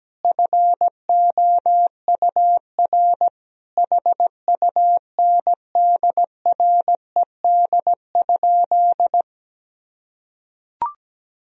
Morse code, heard as a keyed tone, about 17 wpm, 700 Hz. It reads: FOUR HUNDRED?
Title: Sentences from Top 400 Words and Abbreviations 17wpm